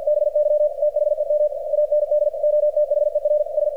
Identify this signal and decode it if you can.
Morse code, heard as a keyed tone, about 48 wpm, 600 Hz, with radio noise. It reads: BY THEM AND OTHERS
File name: archived / CRSnet